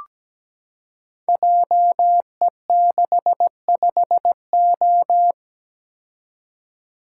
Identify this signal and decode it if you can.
Morse code, heard as a keyed tone, about 17 wpm, 700 Hz. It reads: JE65O
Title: Code Groups - 5 Letter:Numbers - 17wpm